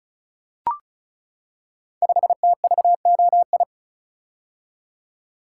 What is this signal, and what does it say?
Morse code, heard as a keyed tone, about 36 wpm, 700 Hz. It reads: 5TVOI
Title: Code Groups - 5 Letter:Numbers - 35wpm